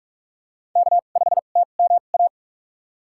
Morse code, heard as a keyed tone, about 46 wpm, spaced as for 32 wpm, 700 Hz. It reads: K5TMA
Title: USA Call Signs - 5 Characters - 2x Character Spacing - 45wpm